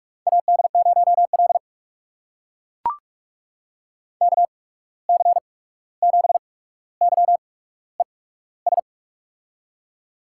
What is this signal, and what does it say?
Morse code, heard as a keyed tone, about 44 wpm, spaced as for 17 wpm, 700 Hz. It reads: AD0L KC7YES
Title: USA Call Signs - 8x Character Spacing - 45wpm